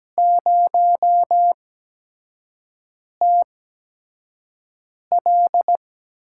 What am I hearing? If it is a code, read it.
Morse code, heard as a keyed tone, about 17 wpm, spaced as for 5 wpm, 700 Hz. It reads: 0TL
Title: USA Call Signs - 4 Characters - 8x Character Spacing - 17wpm